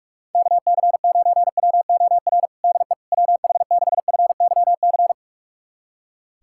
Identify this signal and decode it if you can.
Morse code, heard as a keyed tone, about 46 wpm, 700 Hz. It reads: KC9WOR DE WH6FYC
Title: USA Callsign DE Callsign 45wpm